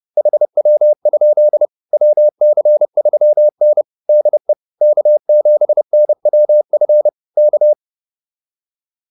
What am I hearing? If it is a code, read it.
Morse code, heard as a keyed tone, about 30 wpm, 600 Hz. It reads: HW? WC3N DE K7NWF K